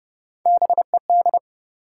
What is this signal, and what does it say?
Morse code, heard as a keyed tone, about 30 wpm, 700 Hz. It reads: BED
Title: Sentences from Top 500 Words - New Word Drill - 30wpm